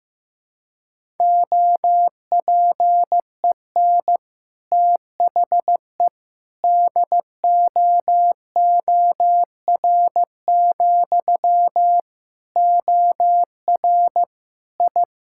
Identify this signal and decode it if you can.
Morse code, heard as a keyed tone, about 15 wpm, 700 Hz. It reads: OPEN THE DOOR, OR I
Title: Sentences from Top 1000 Words - Review Part 6 - 15wpm